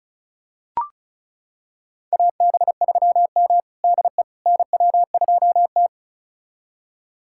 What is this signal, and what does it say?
Morse code, heard as a keyed tone, about 36 wpm, 700 Hz. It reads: AB3M DE NW2T
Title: USA Callsign DE Callsign - 4 Characters - 35wpm